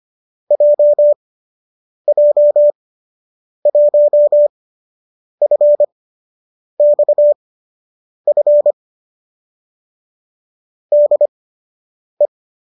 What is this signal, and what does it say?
Morse code, heard as a keyed tone, about 25 wpm, spaced as for 8 wpm, 590 Hz. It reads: JJ1FXF DE